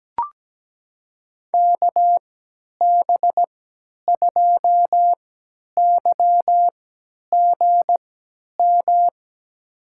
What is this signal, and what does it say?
Morse code, heard as a keyed tone, about 17 wpm, spaced as for 10 wpm, 700 Hz. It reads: KB2YGM
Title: USA Call Signs - 3x Character Spacing - 17wpm